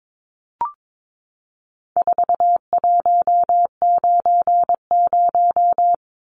Morse code, heard as a keyed tone, about 22 wpm, 700 Hz. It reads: 4190
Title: Numbers - 4-Digit 22wpm